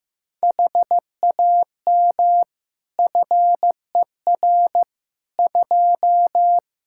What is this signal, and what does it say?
Morse code, heard as a keyed tone, about 16 wpm, 700 Hz. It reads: HAM FER 2
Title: Practice QSO - Exchange 3 - More Info - 15wpm